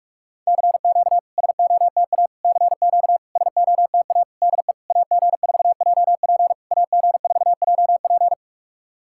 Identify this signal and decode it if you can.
Morse code, heard as a keyed tone, about 44 wpm, 690 Hz. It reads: CQ SOTA CQ SOTA DE AG4JP AG4JP
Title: SOTA QSO Practice 45wpm